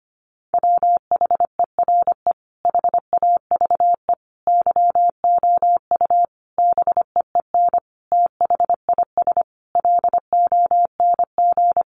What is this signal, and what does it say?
Morse code, heard as a keyed tone, about 25 wpm, 700 Hz. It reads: WHERE HAVE YOU BEEN THIS LONG
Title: Sentences from Top 300 Words 25wpm